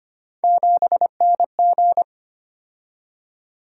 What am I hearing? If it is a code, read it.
Morse code, heard as a keyed tone, about 25 wpm, 700 Hz. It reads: 7NG
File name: Three Letter-Number Combinations - 25wpm